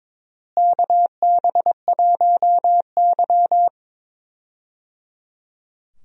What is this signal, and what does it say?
Morse code, heard as a keyed tone, about 22 wpm, 700 Hz. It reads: KB1Y
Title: USA Call Signs - 4 Characters - Repeated 3x - 22wpm